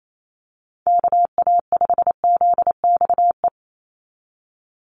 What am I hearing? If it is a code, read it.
Morse code, heard as a keyed tone, about 28 wpm, 700 Hz. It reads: KA5ZXE
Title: USA Call Signs 28wpm